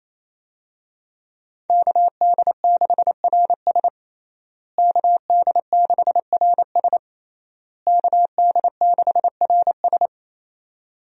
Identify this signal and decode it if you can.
Morse code, heard as a keyed tone, about 28 wpm, 700 Hz. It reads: KD6RS KD6RS KD6RS
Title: USA Call Signs - 5 Characters - Repeated 3x - 28wpm